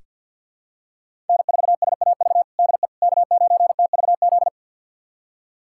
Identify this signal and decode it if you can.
Morse code, heard as a keyed tone, about 47 wpm, 700 Hz. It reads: N4SAV DE K9TVZ
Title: USA Callsign DE Callsign - 5 Characters - 50wpm